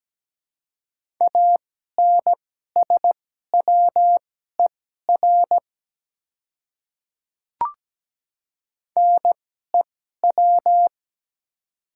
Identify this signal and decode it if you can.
Morse code, heard as a keyed tone, about 17 wpm, spaced as for 12 wpm, 700 Hz. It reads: ANSWER NEW
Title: 2x Character Spacing - Top 300 Words - 17wpm